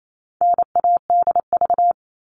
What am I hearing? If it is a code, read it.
Morse code, heard as a keyed tone, about 28 wpm, 700 Hz. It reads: NADV
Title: All Cut Numbers - 4-Digit 28wpm